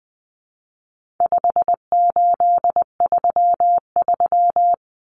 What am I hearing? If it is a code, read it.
Morse code, heard as a keyed tone, about 20 wpm, 700 Hz. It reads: 5833
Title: Numbers - 4-Digit 20wpm